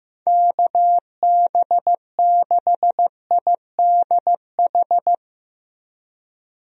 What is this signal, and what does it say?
Morse code, heard as a keyed tone, about 15 wpm, 700 Hz. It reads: KB6IDH
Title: USA Call Signs - 6 Characters - Repeated 2x - 15wpm